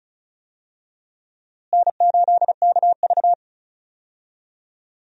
Morse code, heard as a keyed tone, about 36 wpm, 700 Hz. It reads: N8KV
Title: USA Call Signs - 4 Characters - Repeated 3x - 35wpm